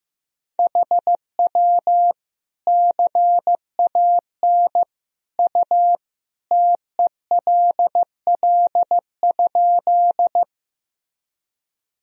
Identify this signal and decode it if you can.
Morse code, heard as a keyed tone, about 15 wpm, 700 Hz. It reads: HW CAN U TELL?
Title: Sentences from Top 200 Words and Abbreviations 15wpm